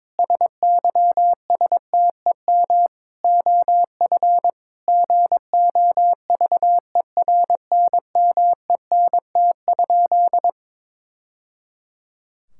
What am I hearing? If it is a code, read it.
Morse code, heard as a keyed tone, about 22 wpm, 700 Hz. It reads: SYSTEM OF GOVERNMENT?
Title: Sentences from Top 1000 Words - Review Part 5 - 22wpm